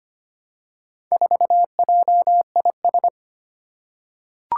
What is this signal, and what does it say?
Morse code, heard as a keyed tone, about 25 wpm, 700 Hz. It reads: 4JIS
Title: Four Challenging Letter-Number Combinations 25wpm